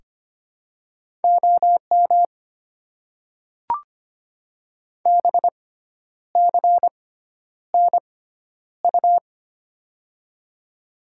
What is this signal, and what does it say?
Morse code, heard as a keyed tone, about 25 wpm, spaced as for 10 wpm, 700 Hz. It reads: OM BCNU
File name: Top 90 QSO Elements - 6x Character Spacing - 25wpm